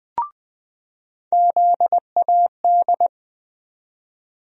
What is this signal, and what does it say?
Morse code, heard as a keyed tone, about 20 wpm, 700 Hz. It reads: ZAD
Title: USA Call Sign Postfixes - 3 Letters 20wpm